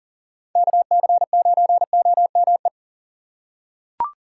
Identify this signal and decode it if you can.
Morse code, heard as a keyed tone, about 38 wpm, 700 Hz. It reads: KC9OME
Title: USA Call Signs - 6 Characters - 4x Character Spacing - 40wpm